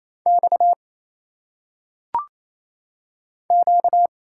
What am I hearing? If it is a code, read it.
Morse code, heard as a keyed tone, about 27 wpm, 700 Hz. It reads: X Q
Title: Single Letters - Phonetic Pronunciation 28wpm